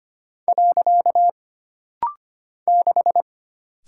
Morse code, heard as a keyed tone, about 25 wpm, 700 Hz. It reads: . 6
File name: Single Character (Letter:Number:Punctuation:Prosign) - Rapid-Fire - 25wpm